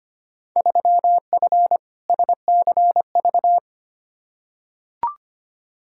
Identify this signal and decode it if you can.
Morse code, heard as a keyed tone, about 25 wpm, 700 Hz. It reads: 3F SCV
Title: Field Day Exchanges - 25wpm